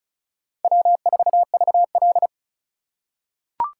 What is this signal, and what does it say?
Morse code, heard as a keyed tone, about 36 wpm, 700 Hz. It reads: W4VL
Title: USA Call Signs - 4 Characters - 2x Character Spacing - 35wpm